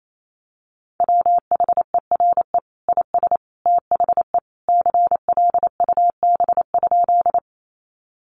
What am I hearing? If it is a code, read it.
Morse code, heard as a keyed tone, about 28 wpm, 700 Hz. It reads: WHERE IS THE CLUB?
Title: Sentences from Top 1200 Words 28wpm